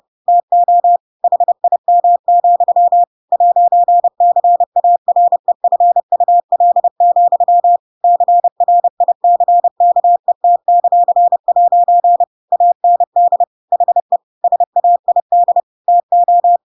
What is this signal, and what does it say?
Morse code, heard as a keyed tone, about 30 wpm, 700 Hz. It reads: TO HIM, 'CAREFUL, CRICKET;' AND HE SAID TO
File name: Pinocchio / Pinocchio ch7 30 30 ik1hge